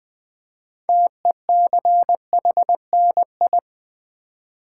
Morse code, heard as a keyed tone, about 20 wpm, 700 Hz. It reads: TECHNI
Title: Build A Word - 3 Letters At A Time - 9-Letter Words - 20wpm